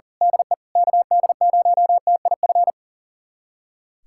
Morse code, heard as a keyed tone, about 40 wpm, 700 Hz. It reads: DE KD0TIF
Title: USA Callsign DE Callsign - 6 Characters - 40wpm